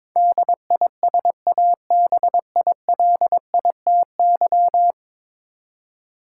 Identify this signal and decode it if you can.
Morse code, heard as a keyed tone, about 22 wpm, 700 Hz. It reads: DISABILITY
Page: morse-code-ninja-practice-22wpm